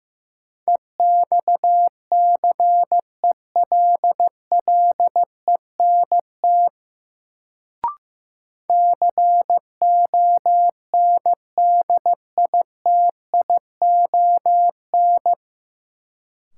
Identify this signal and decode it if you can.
Morse code, heard as a keyed tone, about 15 wpm, 700 Hz. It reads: EXCELLENT CONDITION
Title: Sentences from Top 1000 Words - New Word Drill - 15wpm